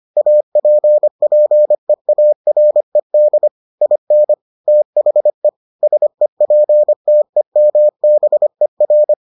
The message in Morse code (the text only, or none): APPEARED IN THE SEPTEMBER